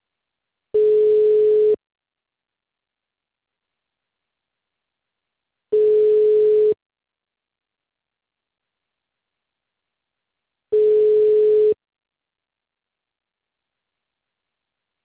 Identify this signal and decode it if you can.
Ringback tone (Europe, 425 Hz).